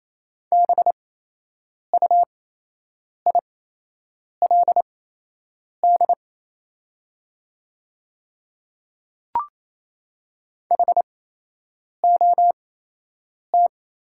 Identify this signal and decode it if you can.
Morse code, heard as a keyed tone, about 28 wpm, spaced as for 8 wpm, 700 Hz. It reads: BUILD HOT
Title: 8x Character Spacing - Top 200 Words - 28wpm